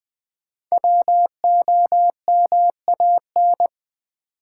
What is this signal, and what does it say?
Morse code, heard as a keyed tone, about 20 wpm, 700 Hz. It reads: WOMAN
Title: Sentences from Top 1100 Words - New Word Drill Alt v2 - 20wpm